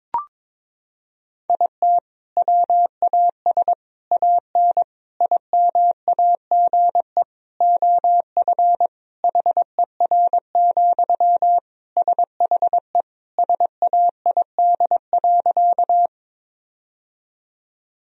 Morse code, heard as a keyed tone, about 22 wpm, 700 Hz. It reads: IT WAS AN IMAGE OF HER, SHE SAID.
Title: Sentences from Top 1000 Words - Review Part 4 - 22wpm